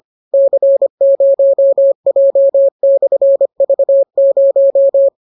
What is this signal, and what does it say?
Morse code, heard as a keyed tone, about 25 wpm, 550 Hz. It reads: C0J/V0